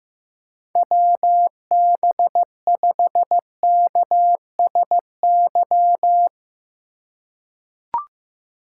Morse code, heard as a keyed tone, about 15 wpm, 700 Hz. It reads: WB5KSY
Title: USA Call Signs - 6 Characters - 4x Character Spacing - 15wpm